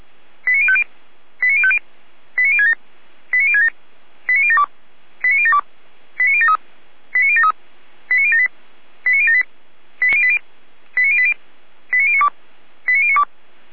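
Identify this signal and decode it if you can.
Five-tone selective call (ZVEI), heard as quick runs of five tones, tones of about 70 ms. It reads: 89050 89050 89067 89067 89072 89072 89073 89073 89078 89078 89080 89092 89092
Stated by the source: ZVEI five-tone selective call received on an SDR.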